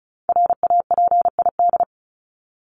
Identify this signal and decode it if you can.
Morse code, heard as a keyed tone, about 34 wpm, 700 Hz. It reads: RAPID
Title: LY Suffix - Top 100 Words with Base Words 35wpm